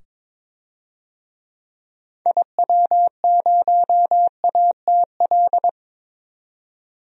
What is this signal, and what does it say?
Morse code, heard as a keyed tone, about 22 wpm, 700 Hz. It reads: IW0ATL
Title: DXCC-100 Call Signs - 6 Characters - 22wpm